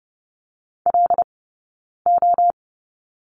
Morse code, heard as a keyed tone, about 30 wpm, 700 Hz. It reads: L O
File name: Lesson 19d - Letter w - Two Characters 30wpm